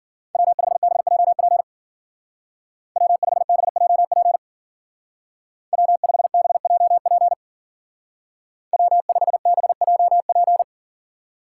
Morse code, heard as a keyed tone, about 49 wpm, 700 Hz. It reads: W5BJP W5BJP W5BJP W5BJP
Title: Speed Racing - USA 1x3 Call Signs 40wpm